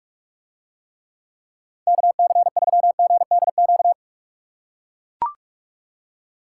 Morse code, heard as a keyed tone, about 41 wpm, 700 Hz. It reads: KC3GDQ